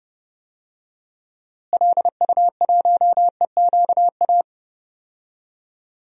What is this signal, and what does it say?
Morse code, heard as a keyed tone, about 30 wpm, 700 Hz. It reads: LU1EQA
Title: DXCC-100 Call Signs - 6 Characters - Repeated 3x - 30wpm